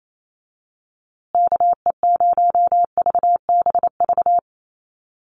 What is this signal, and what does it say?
Morse code, heard as a keyed tone, about 28 wpm, 700 Hz. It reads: KE0VBV
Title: USA Call Signs Repeated 3x 28wpm